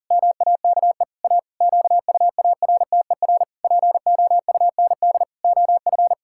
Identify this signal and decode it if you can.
Morse code, heard as a keyed tone, about 40 wpm, 700 Hz. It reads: MAKE A QUARTER POUND OF
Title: Pangram Challenge v2 - 40wpm